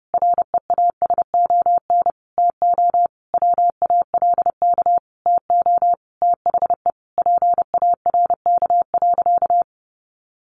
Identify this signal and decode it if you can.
Morse code, heard as a keyed tone, about 30 wpm, 700 Hz. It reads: REASON TO WALK TO THE PARK.
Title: Sentences from Top 1400 Words 30wpm